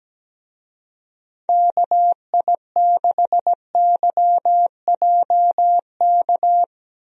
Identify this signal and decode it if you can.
Morse code, heard as a keyed tone, about 17 wpm, 700 Hz. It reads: KI6YJK
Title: USA Call Signs - 6 Characters - Repeated 3x - 17wpm